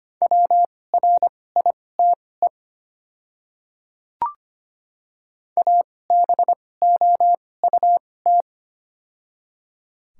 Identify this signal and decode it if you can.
Morse code, heard as a keyed tone, about 25 wpm, spaced as for 18 wpm, 690 Hz. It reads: WRITE ABOUT
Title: 2x Character Spacing - Top 300 Words - 25wpm